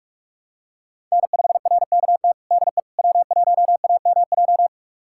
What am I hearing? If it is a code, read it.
Morse code, heard as a keyed tone, about 42 wpm, 700 Hz. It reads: N5RKT DE W1AMJ
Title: USA Callsign DE Callsign - 5 Characters - 45wpm